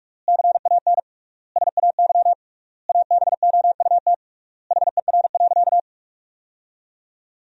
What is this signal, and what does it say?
Morse code, heard as a keyed tone, about 46 wpm, 700 Hz. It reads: CAN SAY ABOUT HER.
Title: 3x Word-Spacing - Sentences from Top 200 Words - 45wpm